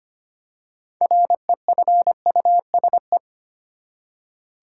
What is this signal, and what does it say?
Morse code, heard as a keyed tone, about 25 wpm, 700 Hz. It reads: REFUSE
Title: Six Letter Words - 25wpm